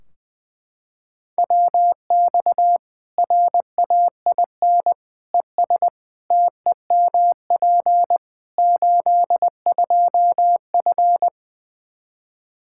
Morse code, heard as a keyed tone, about 21 wpm, 700 Hz. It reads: WX RAIN ES TEMP 82F
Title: Practice QSO - Exchange 2 - Info - 20wpm